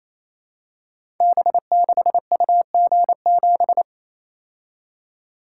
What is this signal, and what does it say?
Morse code, heard as a keyed tone, about 29 wpm, 700 Hz. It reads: B6UG7
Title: Code Groups - 5 Letter:Numbers - 28wpm